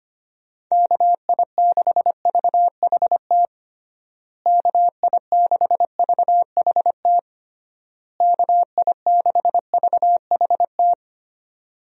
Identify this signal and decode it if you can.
Morse code, heard as a keyed tone, about 25 wpm, 700 Hz. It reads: KI6VHT KI6VHT KI6VHT
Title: USA Call Signs - 6 Characters - Repeated 3x - 25wpm